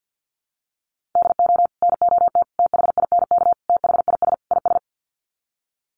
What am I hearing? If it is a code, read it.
Morse code, heard as a keyed tone, about 50 wpm, 700 Hz. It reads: DO NOT THINK THIS IS
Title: Sets of 5 Words Encore2 50wpm